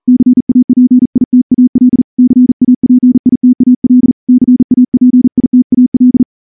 Morse code, heard as a keyed tone, about 38 wpm, 250 Hz. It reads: CAPITAL CAPITAL CAPITAL
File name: morse / morse-training-40WPM